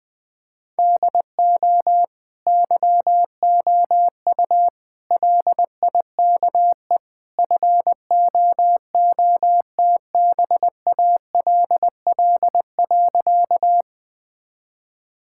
Morse code, heard as a keyed tone, about 20 wpm, 700 Hz. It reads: DO YOU LIKE FOOTBALL.
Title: Sentences from Top 1600 Words 20wpm